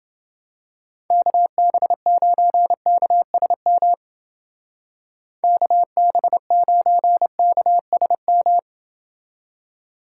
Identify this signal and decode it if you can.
Morse code, heard as a keyed tone, about 29 wpm, 700 Hz. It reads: KB9KSM KB9KSM